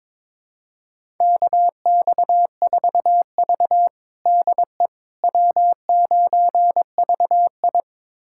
Morse code, heard as a keyed tone, about 21 wpm, 700 Hz. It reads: KX4V DE W9VI
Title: USA Callsign DE Callsign - 4 Characters - 22wpm